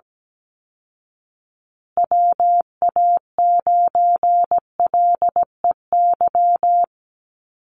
Morse code, heard as a keyed tone, about 17 wpm, 700 Hz. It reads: WA9LEY
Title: Call Signs Repeated 2x - 17wpm